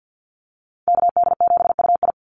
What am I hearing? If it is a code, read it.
Morse code, heard as a keyed tone, about 50 wpm, 700 Hz. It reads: KD7UI
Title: Call Signs Repeated 2x - 50wpm